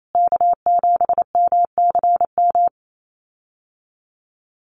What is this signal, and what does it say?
Morse code, heard as a keyed tone, about 28 wpm, 700 Hz. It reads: K7MCM
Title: USA Call Signs Repeated 3x 28wpm